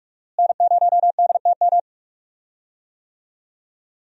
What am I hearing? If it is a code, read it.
Morse code, heard as a keyed tone, about 45 wpm, 700 Hz. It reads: N0DTM
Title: USA Call Signs - 5 Characters - Repeated 3x - 45wpm